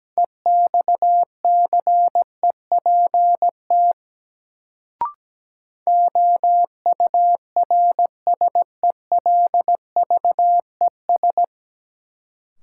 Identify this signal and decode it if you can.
Morse code, heard as a keyed tone, about 17 wpm, 700 Hz. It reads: EXCEPT OURSELVES
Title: Sentences from Top 1600 Words - New Word Drill Alt - 17wpm